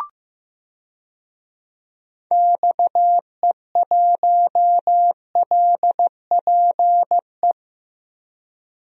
Morse code, heard as a keyed tone, about 16 wpm, 700 Hz. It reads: XE1LPE
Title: DXCC-100 Call Signs - 6 Characters - Repeated 3x - 15wpm